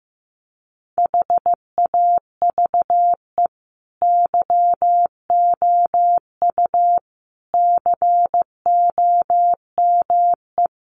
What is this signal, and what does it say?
Morse code, heard as a keyed tone, about 15 wpm, 700 Hz. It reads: HAVE YOU COME
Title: Sentences from Top 1400 Words 15wpm